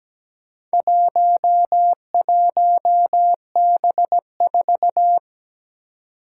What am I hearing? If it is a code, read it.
Morse code, heard as a keyed tone, about 17 wpm, 700 Hz. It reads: 11B4
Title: Four Challenging Letter-Number Combinations 17wpm